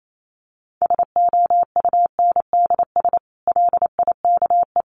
Morse code, heard as a keyed tone, about 28 wpm, 700 Hz. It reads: SOUNDS LIKE
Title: A Sound Of Thunder 28wpm